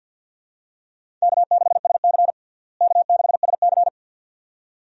Morse code, heard as a keyed tone, about 52 wpm, 700 Hz. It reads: K6SC K6SC
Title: USA Call Signs - 4 Characters - Repeated 2x - 50wpm